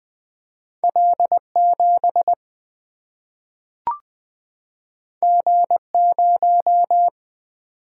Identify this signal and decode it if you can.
Morse code, heard as a keyed tone, about 20 wpm, 700 Hz. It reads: L 7 G 0
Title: Single-Letter Single-Number 20wpm